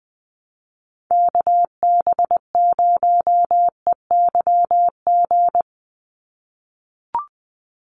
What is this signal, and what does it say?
Morse code, heard as a keyed tone, about 20 wpm, 700 Hz. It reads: KB0EYG
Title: USA Call Signs 20wpm